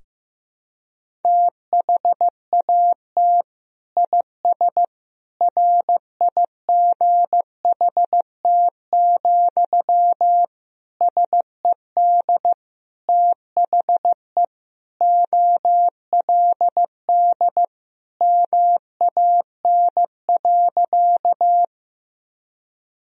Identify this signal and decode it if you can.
Morse code, heard as a keyed tone, about 15 wpm, 700 Hz. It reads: THAT IS RIGHT, SED THE OLD MAN.
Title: Sentences from Top 200 Words and Abbreviations 15wpm